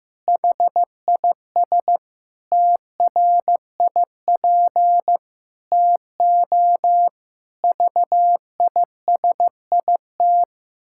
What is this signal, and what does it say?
Morse code, heard as a keyed tone, about 15 wpm, 700 Hz. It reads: HIS TRIP TO VISIT
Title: Pangram Challenge v2 - 15wpm